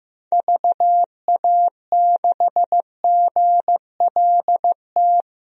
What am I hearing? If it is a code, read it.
Morse code, heard as a keyed tone, about 15 wpm, 700 Hz. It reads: VA6GLT
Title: DXCC-100 Call Signs - 6 Characters - Repeated 2x - 15wpm